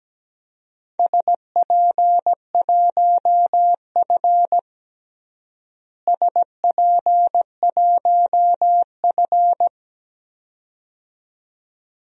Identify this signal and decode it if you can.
Morse code, heard as a keyed tone, about 17 wpm, 700 Hz. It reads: SP1F SP1F
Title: DXCC-100 Call Signs - 4 Characters - Repeated 2x - 17wpm